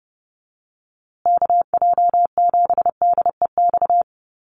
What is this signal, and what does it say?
Morse code, heard as a keyed tone, about 30 wpm, 700 Hz. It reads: KJ7DEX
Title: USA Call Signs Repeated 3x 30wpm